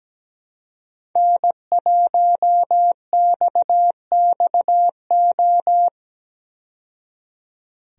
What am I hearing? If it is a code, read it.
Morse code, heard as a keyed tone, about 17 wpm, 700 Hz. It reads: N1XXO
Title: USA Call Signs - 5 Characters - 17wpm